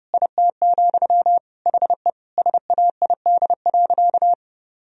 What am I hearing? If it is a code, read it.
Morse code, heard as a keyed tone, about 30 wpm, 700 Hz. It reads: IT, HE SAID.
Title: Sentences from Top 1000 Words - Review Part 4 - 30wpm